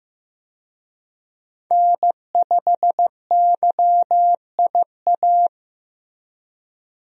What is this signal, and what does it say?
Morse code, heard as a keyed tone, about 15 wpm, 700 Hz. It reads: N5YIA
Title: USA Call Signs - 5 Characters - Repeated 3x - 15wpm